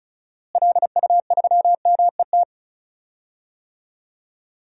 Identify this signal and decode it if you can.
Morse code, heard as a keyed tone, about 36 wpm, 700 Hz. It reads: LU3MET
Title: DXCC-100 Call Signs - 6 Characters - Repeated 3x - 35wpm